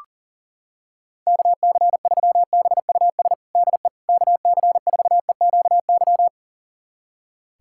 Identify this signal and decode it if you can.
Morse code, heard as a keyed tone, about 40 wpm, 700 Hz. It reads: KC3BUS DE KC4EQY
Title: USA Callsign DE Callsign 40wpm